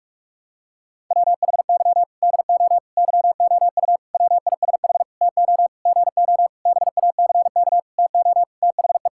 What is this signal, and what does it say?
Morse code, heard as a keyed tone, about 46 wpm, 700 Hz. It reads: WHY DO YOU WISH TO GO BACK TO THE
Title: Sentences from Top 1600 Words 45wpm